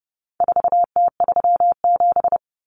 Morse code, heard as a keyed tone, about 30 wpm, 700 Hz. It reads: 4T37
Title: Common Cut Numbers - 4-Digit 30wpm